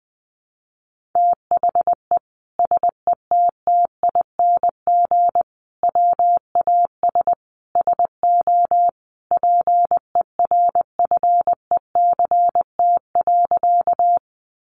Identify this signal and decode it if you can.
Morse code, heard as a keyed tone, about 20 wpm, 700 Hz. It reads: THE SETTING WAS SO PERFECT.
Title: Sentences from Top 1300 Words 20wpm